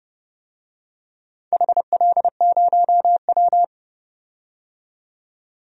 Morse code, heard as a keyed tone, about 30 wpm, 700 Hz. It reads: HL0W
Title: DXCC-100 Call Signs - 4 Characters - 30wpm